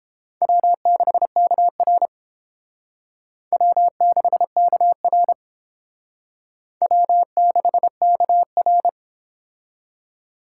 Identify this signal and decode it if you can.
Morse code, heard as a keyed tone, about 31 wpm, 700 Hz. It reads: W6KR W6KR W6KR